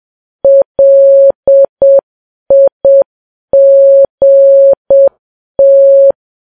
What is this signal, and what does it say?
Morse code, heard as a keyed tone, about 7 wpm, 550 Hz. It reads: LIGT
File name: Files Web tls / Morse on9fv / morse7kps